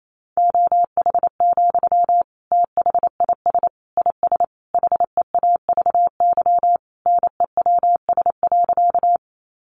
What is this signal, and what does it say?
Morse code, heard as a keyed tone, about 28 wpm, 700 Hz. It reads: OH, THIS IS HEAVY NEWS.
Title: Sentences from Top 1100 Words 28wpm